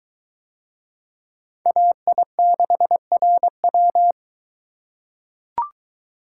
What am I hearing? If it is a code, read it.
Morse code, heard as a keyed tone, about 22 wpm, 700 Hz. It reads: AI6RW